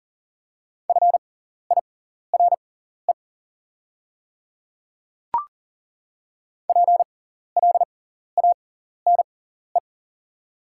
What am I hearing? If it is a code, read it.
Morse code, heard as a keyed tone, about 40 wpm, spaced as for 14 wpm, 690 Hz. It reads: FIRE PLANE
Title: 6x Character Spacing - Top 500 Words - 40wpm